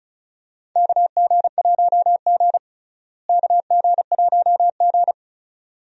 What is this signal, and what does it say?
Morse code, heard as a keyed tone, about 34 wpm, 700 Hz. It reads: KG1G KG1G
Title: USA Call Signs - 4 Characters - Repeated 2x - 35wpm